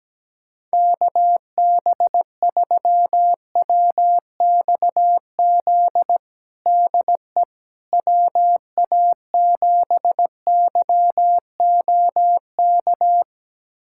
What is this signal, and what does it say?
Morse code, heard as a keyed tone, about 17 wpm, 700 Hz. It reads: KB3WXZ DE WA7YOK
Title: USA Callsign DE Callsign - 6 Characters - 17wpm